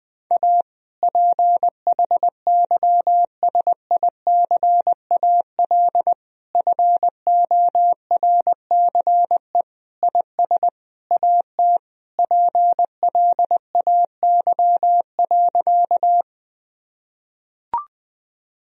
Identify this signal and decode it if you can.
Morse code, heard as a keyed tone, about 20 wpm, 700 Hz. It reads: A PHYSICAL FORCE IS AT PLAY.
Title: Sentences from Top 1000 Words - Review Part 4 - 20wpm